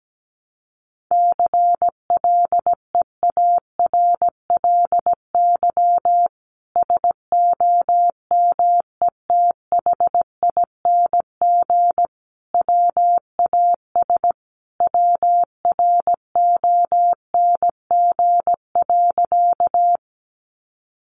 Morse code, heard as a keyed tone, about 17 wpm, 700 Hz. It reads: CLEARLY SOMETHING WAS WRONG.
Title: Sentences from Top 1300 Words 17wpm